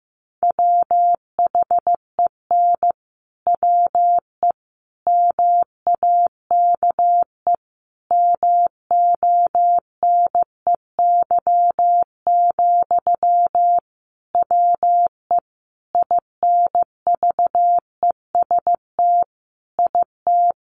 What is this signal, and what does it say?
Morse code, heard as a keyed tone, about 15 wpm, 700 Hz. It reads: WHEN WE MAKE MONEY, WE INVEST IT
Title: Sentences from Top 1100 Words 15wpm